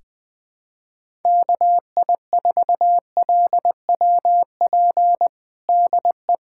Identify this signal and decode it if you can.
Morse code, heard as a keyed tone, about 20 wpm, 700 Hz. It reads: KI4LWP DE
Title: USA Callsign DE Callsign 20wpm